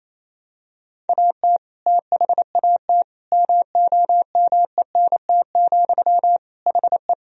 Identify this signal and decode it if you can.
Morse code, heard as a keyed tone, about 28 wpm, 700 Hz. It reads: AT THAT MOMENT, HE
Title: Sentences from Top 1000 Words - Review Part 1 - 28wpm